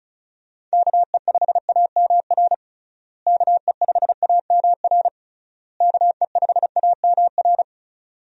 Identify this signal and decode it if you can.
Morse code, heard as a keyed tone, about 36 wpm, 700 Hz. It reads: KE5AMR KE5AMR KE5AMR
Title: USA Call Signs - 6 Characters - Repeated 3x - 35wpm